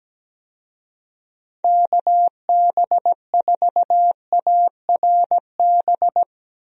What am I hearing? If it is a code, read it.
Morse code, heard as a keyed tone, about 17 wpm, 700 Hz. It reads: KB4ARB
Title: USA Call Signs - 6 Characters - 17wpm